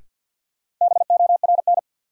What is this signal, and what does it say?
Morse code, heard as a keyed tone, about 53 wpm, 700 Hz. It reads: BORN
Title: Sentences from Top 1000 Words - New Word Drill - 50wpm